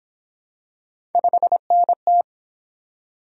Morse code, heard as a keyed tone, about 31 wpm, 700 Hz. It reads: 5NT